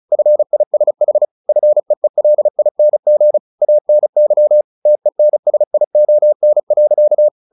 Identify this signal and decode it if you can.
Morse code, heard as a keyed tone, about 34 wpm, 600 Hz. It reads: FISH FEELING ANY TENSION.
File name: the old man and the sea / the old man and the sea 1 35wpm